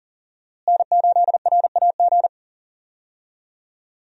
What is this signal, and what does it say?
Morse code, heard as a keyed tone, about 40 wpm, 700 Hz. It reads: N8RAG
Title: USA Call Signs - 5 Characters - Repeated 2x - 40wpm